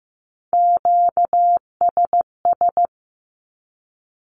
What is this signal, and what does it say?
Morse code, heard as a keyed tone, about 15 wpm, 700 Hz. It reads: QSS
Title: Q-Codes 15wpm